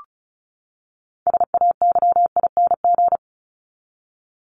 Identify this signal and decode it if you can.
Morse code, heard as a keyed tone, about 34 wpm, 700 Hz. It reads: SAYING
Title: ING Suffix - Top 100 Words with Base Words 35wpm